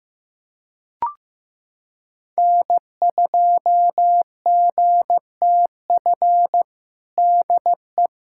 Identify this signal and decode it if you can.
Morse code, heard as a keyed tone, about 15 wpm, 700 Hz. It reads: N2GTF DE
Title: Practice QSO - Exchange 4 - Ending - 15wpm